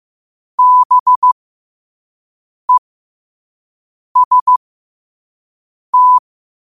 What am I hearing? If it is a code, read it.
Morse code, heard as a keyed tone, about 15 wpm, spaced as for 5 wpm, 1000 Hz. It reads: BEST
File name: qotdfiles / 5wpm